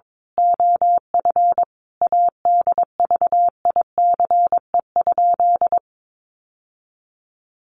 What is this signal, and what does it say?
Morse code, heard as a keyed tone, about 22 wpm, 700 Hz. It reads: OF ADVICE?
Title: Sentences from Top 1100 Words 22wpm